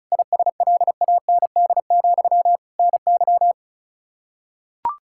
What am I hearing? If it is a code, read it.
Morse code, heard as a keyed tone, about 34 wpm, 700 Hz. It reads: ISLAND, NY
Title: Practice QSO - Exchange 1 - Beginning - 35wpm